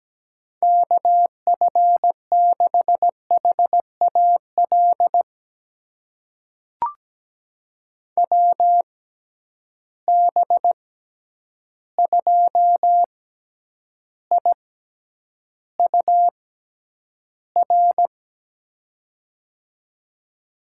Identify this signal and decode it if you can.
Morse code, heard as a keyed tone, about 17 wpm, spaced as for 9 wpm, 700 Hz. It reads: KF6HAL WB2IUR
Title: USA Call Signs - 6 Characters - 6x Character Spacing - 17wpm